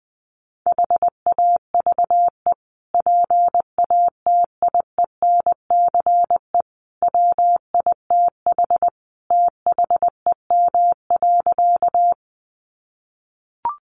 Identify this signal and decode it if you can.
Morse code, heard as a keyed tone, about 20 wpm, 700 Hz. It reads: HAVE PATIENCE WITH THEM.
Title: Sentences from Top 1300 Words 20wpm